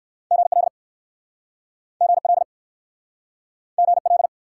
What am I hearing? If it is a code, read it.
Morse code, heard as a keyed tone, about 61 wpm, 700 Hz. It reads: GL GL GL